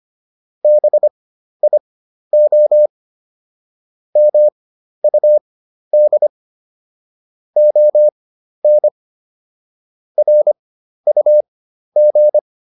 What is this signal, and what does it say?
Morse code, heard as a keyed tone, about 25 wpm, spaced as for 12 wpm, 600 Hz. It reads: BIO MUD ON RUG